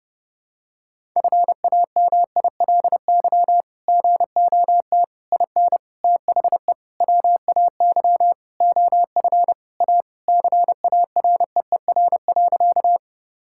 Morse code, heard as a keyed tone, about 32 wpm, 700 Hz. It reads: FAMILY GOT IN THE WAY OF A CAREER.
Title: Sentences from Top 1000 Words - Review Part 6 - 30wpm